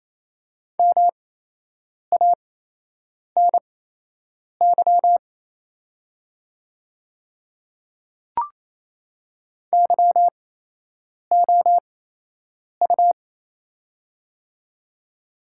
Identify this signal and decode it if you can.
Morse code, heard as a keyed tone, about 28 wpm, spaced as for 8 wpm, 700 Hz. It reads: MANY YOU
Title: 8x Character Spacing - Top 300 Words - 28wpm